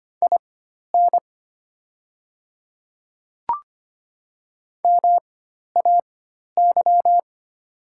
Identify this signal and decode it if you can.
Morse code, heard as a keyed tone, about 25 wpm, spaced as for 12 wpm, 700 Hz. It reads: IN MAY
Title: 4x Character Spacing - Top 100 Words - 25wpm